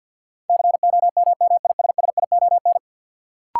Morse code, heard as a keyed tone, about 52 wpm, 700 Hz. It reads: COMMISSION
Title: Sentences from Top 1400 Words - New Word Drill Alt - 50wpm